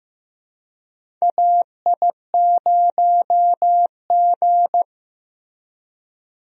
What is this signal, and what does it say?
Morse code, heard as a keyed tone, about 15 wpm, 700 Hz. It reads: AI0G
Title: USA Call Signs - 4 Characters - Repeated 3x - 15wpm